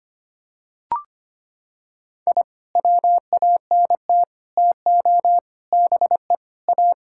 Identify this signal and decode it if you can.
Morse code, heard as a keyed tone, about 25 wpm, 700 Hz. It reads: I WANT TO BE A
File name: practice / 25wpm / Sentences with 3-Word Phrases - 25wpm